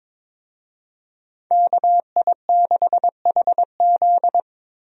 Morse code, heard as a keyed tone, about 21 wpm, 700 Hz. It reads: KI6HZ
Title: USA Call Signs - 5 Characters - Repeated 3x - 22wpm